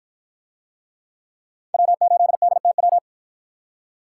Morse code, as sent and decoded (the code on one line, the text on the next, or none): .-- ---.. -.. - .--
W8DTW